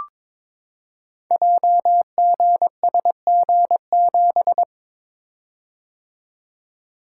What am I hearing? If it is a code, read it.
Morse code, heard as a keyed tone, about 22 wpm, 700 Hz. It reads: JGSG7
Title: Code Groups - 5 Letter:Numbers - 22wpm